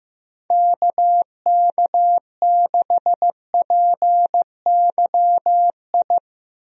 Morse code, heard as a keyed tone, about 15 wpm, 700 Hz. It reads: KK6PYI
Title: USA Call Signs - 6 Characters - Repeated 3x - 15wpm